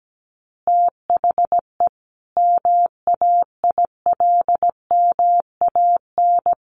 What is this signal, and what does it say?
Morse code, heard as a keyed tone, about 17 wpm, 700 Hz. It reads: THE MAILMAN
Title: Easy Phrases 17wpm